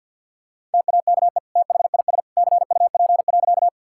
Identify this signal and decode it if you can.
Morse code, heard as a keyed tone, about 50 wpm, 700 Hz. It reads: TAKE THIS CUP.
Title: Sentences from Top 1600 Words 50wpm